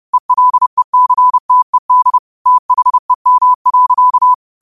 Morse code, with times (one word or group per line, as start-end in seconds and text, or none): 0.1-2.2 ELECTED
2.5-4.3 THEM.